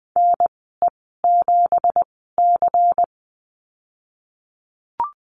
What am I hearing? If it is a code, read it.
Morse code, heard as a keyed tone, about 19 wpm, spaced as for 14 wpm, 700 Hz. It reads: NE7C
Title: 173 Lesson 28g - Number 9 - Call Signs 20wpm